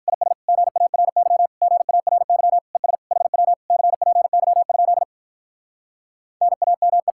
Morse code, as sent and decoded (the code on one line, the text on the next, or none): .. ... / --. .- .-. -.-- / --. .- .-. -.-- / . ... / .... .-- / -.-. .--. -.-- ..--.. / -. .- -- .
IS GARY GARY ES HW CPY? NAME